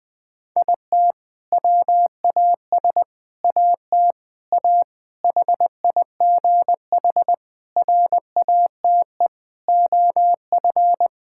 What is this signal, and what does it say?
Morse code, heard as a keyed tone, about 20 wpm, 700 Hz. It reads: IT WAS AT A HIGH RATE OF
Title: Sentences from Top 1000 Words - Review Part 5 - 20wpm